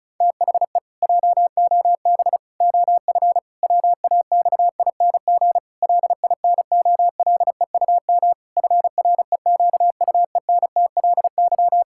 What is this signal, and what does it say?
Morse code, heard as a keyed tone, about 36 wpm, 700 Hz. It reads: THE JOB OF WAXING LINOLEUM FREQUENTLY
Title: Pangram Challenge v2 - 35wpm